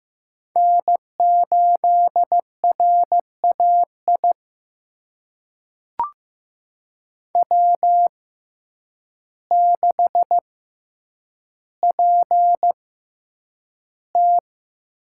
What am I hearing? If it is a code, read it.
Morse code, heard as a keyed tone, about 15 wpm, spaced as for 8 wpm, 700 Hz. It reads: N8RAI W6PT
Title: USA Call Signs - 5 Characters - 6x Character Spacing - 15wpm